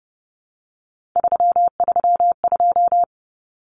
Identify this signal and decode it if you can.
Morse code, heard as a keyed tone, about 30 wpm, 700 Hz. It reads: 332
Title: Common Cut Numbers - 3-Digit 30wpm